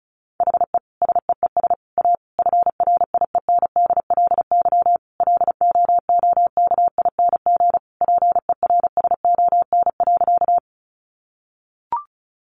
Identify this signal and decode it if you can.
Morse code, heard as a keyed tone, about 34 wpm, 700 Hz. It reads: HE SEES A FRIENDLY LOOKING PERSON.
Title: Easy Phrases 35wpm